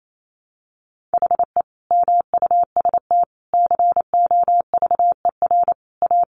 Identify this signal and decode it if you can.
Morse code, heard as a keyed tone, about 28 wpm, 700 Hz. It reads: HE MUST COVER A
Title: Sentences from Top 500 Words 28wpm